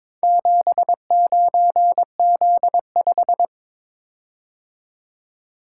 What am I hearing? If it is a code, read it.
Morse code, heard as a keyed tone, about 22 wpm, 700 Hz. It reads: 79Z5
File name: practice / 22wpm / Four Letter-Number Combinations - 22wpm